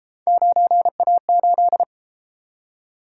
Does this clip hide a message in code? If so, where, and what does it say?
Yes: Morse code from 0.3 to 1.8 s, heard as a keyed tone, about 33 wpm, 700 Hz. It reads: 9A8